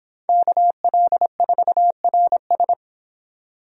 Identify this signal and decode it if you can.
Morse code, heard as a keyed tone, about 25 wpm, 700 Hz. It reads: KL4RS